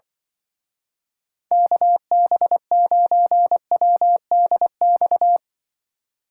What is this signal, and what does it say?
Morse code, heard as a keyed tone, about 24 wpm, 700 Hz. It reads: KB9WDX
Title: Speed Racing - USA Call Signs 20wpm